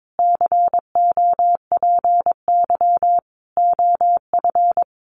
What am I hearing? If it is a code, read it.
Morse code, heard as a keyed tone, about 22 wpm, 700 Hz. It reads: COPY OF
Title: Sentences from Top 1400 Words 22wpm